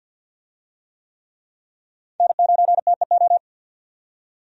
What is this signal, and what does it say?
Morse code, heard as a keyed tone, about 52 wpm, 700 Hz. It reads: N9TEO
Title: USA Call Signs - 5 Characters - 50wpm